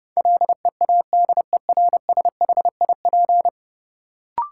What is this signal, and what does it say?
Morse code, heard as a keyed tone, about 32 wpm, 700 Hz. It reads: LEADERSHIP
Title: Sentences from Top 1600 Words - New Word Drill - 30wpm